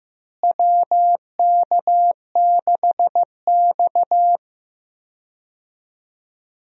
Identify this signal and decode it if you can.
Morse code, heard as a keyed tone, about 15 wpm, 700 Hz. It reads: WK6X
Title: USA Call Signs - 4 Characters - Repeated 2x - 15wpm